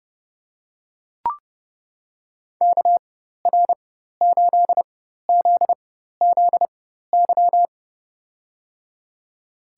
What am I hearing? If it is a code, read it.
Morse code, heard as a keyed tone, about 31 wpm, spaced as for 14 wpm, 700 Hz. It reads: KR8ZZY
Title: USA Call Signs - 4x Character Spacing - 30wpm